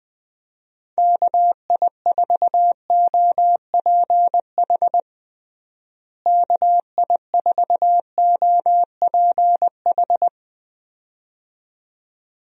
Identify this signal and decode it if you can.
Morse code, heard as a keyed tone, about 20 wpm, 700 Hz. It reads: KI4OPH KI4OPH
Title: USA Call Signs - 6 Characters - Repeated 2x - 20wpm